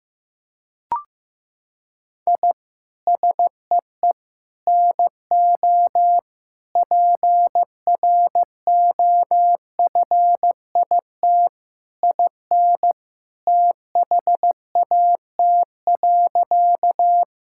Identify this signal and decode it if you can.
Morse code, heard as a keyed tone, about 15 wpm, 700 Hz. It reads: I SEE NO PROFIT IN THAT.
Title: Sentences from Top 1000 Words - Review Part 5 - 15wpm